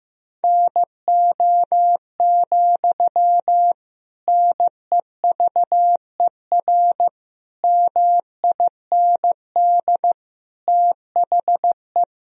Morse code, heard as a keyed tone, about 16 wpm, 700 Hz. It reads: NO, NEVER MIND THE
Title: Sentences from Top 1000 Words - Review Part 1 - 15wpm